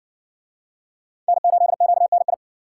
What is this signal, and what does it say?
Morse code, heard as a keyed tone, about 60 wpm, 700 Hz. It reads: N8QNI